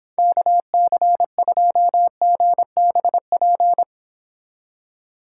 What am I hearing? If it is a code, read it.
Morse code, heard as a keyed tone, about 26 wpm, 700 Hz. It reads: KC2GBP